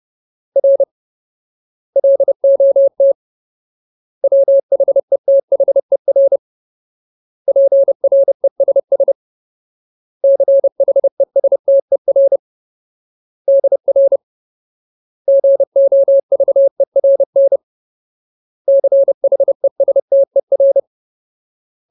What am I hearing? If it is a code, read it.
Morse code, heard as a keyed tone, about 30 wpm, 550 Hz. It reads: R LOT WHETHER PRESS CHESTER DR GOVERN CHESTER
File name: Ultimate Word List / 30WPM Ultimate